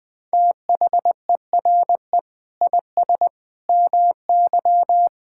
Morse code, heard as a keyed tone, about 21 wpm, 700 Hz. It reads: THERE IS MY
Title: Sentences from Top 1000 Words - Review Part 6 - 20wpm